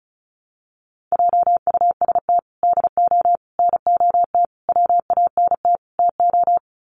Morse code, heard as a keyed tone, about 34 wpm, 700 Hz. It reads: JUST DO NOT WANT TO
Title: Sets of 5 Words Encore 35wpm